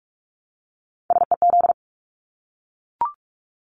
Morse code, heard as a keyed tone, about 44 wpm, 700 Hz. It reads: SEZ
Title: Abbreviations Repeated 3x - 45wpm